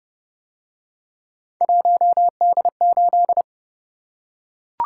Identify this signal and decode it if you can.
Morse code, heard as a keyed tone, about 29 wpm, 700 Hz. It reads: 1D8